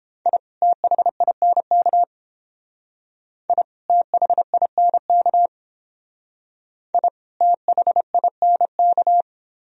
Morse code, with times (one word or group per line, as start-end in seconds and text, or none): 0.3-0.4 I
0.6-2.0 THINK
3.5-3.6 I
3.9-5.5 THINK
6.9-7.1 I
7.4-9.2 THINK